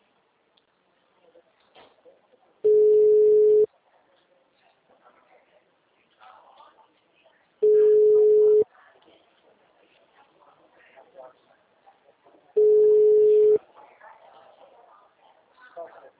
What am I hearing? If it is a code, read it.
Ringback tone (Europe, 425 Hz), with speech.